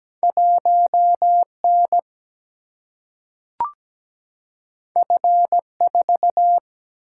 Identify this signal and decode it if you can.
Morse code, heard as a keyed tone, about 17 wpm, 700 Hz. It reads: 1N F4
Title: Single-Letter Single-Number 17wpm